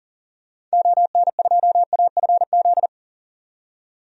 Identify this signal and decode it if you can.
Morse code, heard as a keyed tone, about 40 wpm, 700 Hz. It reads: ON2AFZ
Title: DXCC-100 Call Signs - 6 Characters - 40wpm